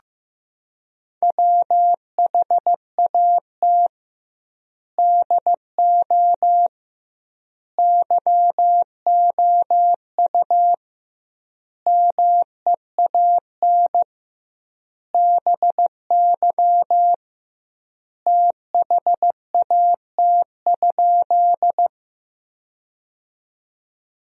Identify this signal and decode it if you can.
Morse code, heard as a keyed tone, about 15 wpm, 700 Hz. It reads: WHAT DO YOU MEAN BY THAT?
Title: 2x Word-Spacing - Sentences from Top 200 Words - 15wpm